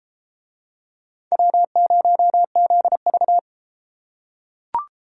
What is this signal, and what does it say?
Morse code, heard as a keyed tone, about 31 wpm, 700 Hz. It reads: W0ZV